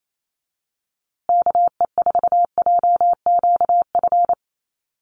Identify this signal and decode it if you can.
Morse code, heard as a keyed tone, about 28 wpm, 700 Hz. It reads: KE4JQF
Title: USA Call Signs Repeated 3x 28wpm